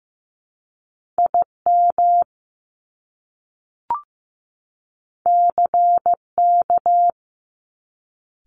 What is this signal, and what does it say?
Morse code, heard as a keyed tone, about 15 wpm, 700 Hz. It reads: IM CK
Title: Top Two Letter Combinations in Words 15wpm